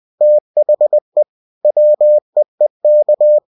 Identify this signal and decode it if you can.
Morse code, heard as a keyed tone, about 20 wpm, 600 Hz. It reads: THE WEEK
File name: cw-books / winnie-the-pooh / winnie-20wpm-page-0019